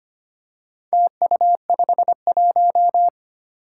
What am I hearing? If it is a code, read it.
Morse code, heard as a keyed tone, about 25 wpm, 700 Hz. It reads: TU51
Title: Four Challenging Letter-Number Combinations 25wpm